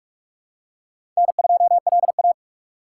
Morse code, heard as a keyed tone, about 44 wpm, 700 Hz. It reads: N2LA
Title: USA Call Signs - 4 Characters - 2x Character Spacing - 45wpm